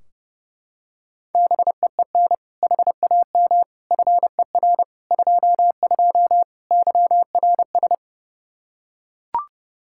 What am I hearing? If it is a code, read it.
Morse code, heard as a keyed tone, about 32 wpm, 700 Hz. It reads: BEEN HAM FER 22 YRS
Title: Practice QSO - Exchange 3 - More Info - 30wpm